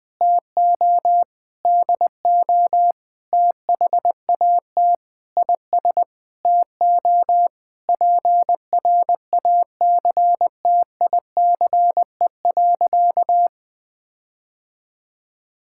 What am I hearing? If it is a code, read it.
Morse code, heard as a keyed tone, about 20 wpm, 700 Hz. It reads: TO DO THAT IS TO PRACTICE.
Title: Sentences with 9-Word Phrases - Spoken Beginning - 20wpm